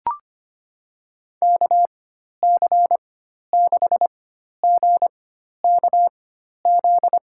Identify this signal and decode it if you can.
Morse code, heard as a keyed tone, about 25 wpm, spaced as for 12 wpm, 700 Hz. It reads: KC6GKZ
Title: USA Call Signs - 6 Characters - 4x Character Spacing - 25wpm